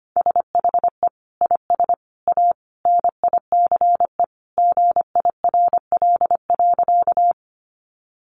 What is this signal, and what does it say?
Morse code, heard as a keyed tone, about 25 wpm, 700 Hz. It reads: SHE IS A NICE GIRL.
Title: Sentences from Top 1100 Words 25wpm